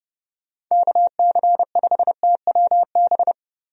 Morse code, heard as a keyed tone, about 30 wpm, 700 Hz. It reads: KC5TWB
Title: USA Call Signs - 6 Characters - Repeated 2x - 30wpm